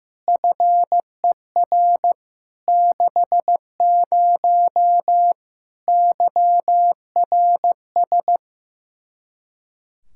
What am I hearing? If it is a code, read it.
Morse code, heard as a keyed tone, about 15 wpm, 700 Hz. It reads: FER 60 YRS
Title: Practice QSO - Exchange 3 - More Info - 15wpm